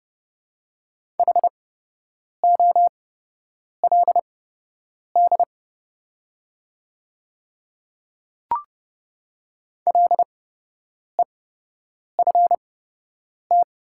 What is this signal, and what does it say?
Morse code, heard as a keyed tone, about 31 wpm, spaced as for 8 wpm, 700 Hz. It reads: HOLD LEFT
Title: 8x Character Spacing - Top 500 Words - 30wpm